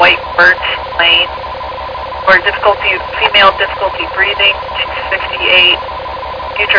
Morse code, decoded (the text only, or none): none